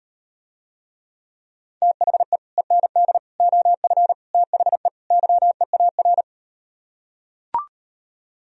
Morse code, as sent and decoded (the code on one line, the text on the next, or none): - .... / . -. -.. / --- ..-. / - .... / -.-- . .- .-.
THE END OF THE YEAR